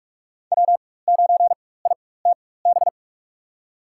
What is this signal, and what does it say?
Morse code, heard as a keyed tone, about 45 wpm, spaced as for 21 wpm, 700 Hz. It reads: W9ITB
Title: USA Call Signs - 5 Characters - 4x Character Spacing - 45wpm